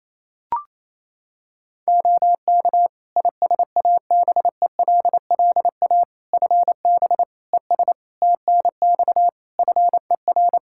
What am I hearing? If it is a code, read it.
Morse code, heard as a keyed tone, about 28 wpm, 700 Hz. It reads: OK ISABELLA FB ES TNX FER
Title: Practice QSO - Exchange 2 - Info - 28wpm